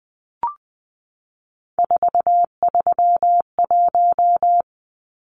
Morse code, heard as a keyed tone, about 20 wpm, 700 Hz. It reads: 431
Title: Numbers - 3-Digit 20wpm